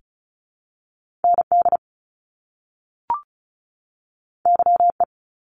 Morse code, heard as a keyed tone, about 36 wpm, 700 Hz. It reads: ND YE
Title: Top Two Letter Combinations in Words 35wpm